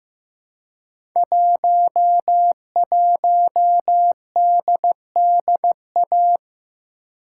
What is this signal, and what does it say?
Morse code, heard as a keyed tone, about 15 wpm, 700 Hz. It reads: 11DDA
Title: Code Groups - 5 Letter:Numbers - 15wpm